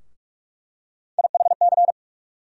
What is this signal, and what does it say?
Morse code, heard as a keyed tone, about 45 wpm, 700 Hz. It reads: IHC
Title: USA Call Sign Postfixes - 3 Letters 45wpm